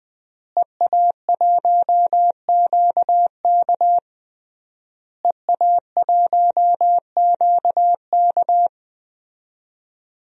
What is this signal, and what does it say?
Morse code, heard as a keyed tone, about 20 wpm, 700 Hz. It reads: EA1QK EA1QK
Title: DXCC-100 Call Signs - 5 Characters - Repeated 3x - 20wpm